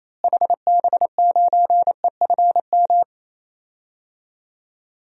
Morse code, heard as a keyed tone, about 28 wpm, 700 Hz. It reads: HB9EFM
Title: DXCC-100 Call Signs - 6 Characters - Repeated 2x - 28wpm